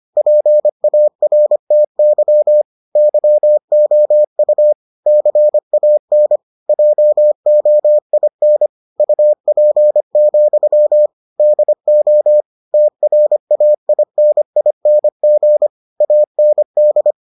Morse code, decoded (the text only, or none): PARTY YOU CAN JOIN UP, DO TRAINING AND